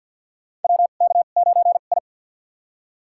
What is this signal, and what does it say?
Morse code, heard as a keyed tone, about 47 wpm, spaced as for 35 wpm, 700 Hz. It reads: WK9I
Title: USA Call Signs - 4 Characters - 2x Character Spacing - 50wpm